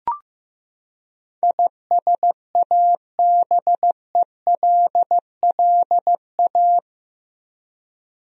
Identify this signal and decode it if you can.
Morse code, heard as a keyed tone, about 15 wpm, 700 Hz. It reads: ISABELLA
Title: USA Common Names - 15wpm